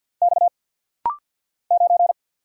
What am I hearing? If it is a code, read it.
Morse code, heard as a keyed tone, about 50 wpm, 700 Hz. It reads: X 9
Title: Single Letter-Number Alt - Rapid-Fire 50wpm